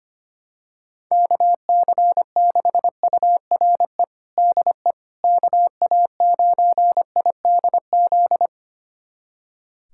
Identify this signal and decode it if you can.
Morse code, heard as a keyed tone, about 25 wpm, 700 Hz. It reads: KC6URE DE KA9IDZ
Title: USA Callsign DE Callsign - 6 Characters - 25wpm